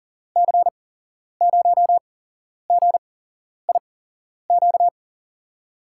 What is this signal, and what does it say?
Morse code, heard as a keyed tone, about 39 wpm, 700 Hz. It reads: C 0 G I Q
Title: USA Call Signs - 6 Characters - 8x Character Spacing - 40wpm